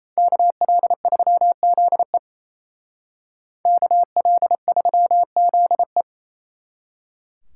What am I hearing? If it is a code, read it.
Morse code, heard as a keyed tone, about 36 wpm, 700 Hz. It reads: KL3ZE KL3ZE